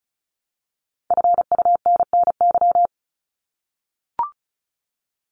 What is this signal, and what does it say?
Morse code, heard as a keyed tone, about 34 wpm, 700 Hz. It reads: FUNNY
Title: Top 1751-2000 Words 35wpm